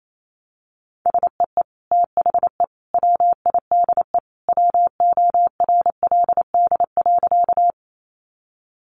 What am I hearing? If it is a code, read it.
Morse code, heard as a keyed tone, about 28 wpm, 700 Hz. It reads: SEE THE WIDE WORLD.
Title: Sentences from Top 900 Words 28wpm